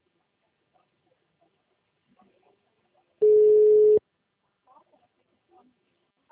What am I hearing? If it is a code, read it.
Ringback tone (Europe, 425 Hz).